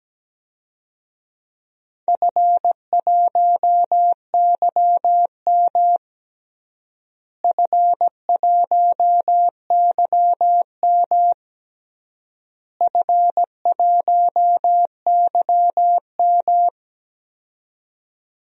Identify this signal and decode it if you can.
Morse code, heard as a keyed tone, about 17 wpm, 700 Hz. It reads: F1YM F1YM F1YM
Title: DXCC-100 Call Signs - 4 Characters - Repeated 3x - 17wpm